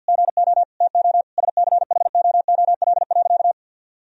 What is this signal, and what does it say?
Morse code, heard as a keyed tone, about 53 wpm, 700 Hz. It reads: GO TO SCHOOL.
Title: Sentences from Top 1000 Words - Review Part 2 - 50wpm